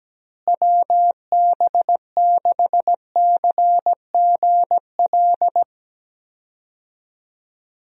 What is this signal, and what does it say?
Morse code, heard as a keyed tone, about 17 wpm, 700 Hz. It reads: WB6CGL
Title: USA Call Signs - 6 Characters - Repeated 2x - 17wpm